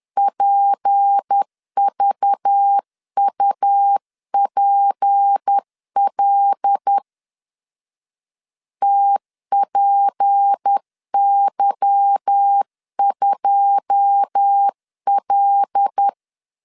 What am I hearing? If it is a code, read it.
Morse code, heard as a keyed tone, about 10 wpm, 800 Hz. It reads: PVUPL TPY2L
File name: LEZIONI / 9 28 10wpm 2842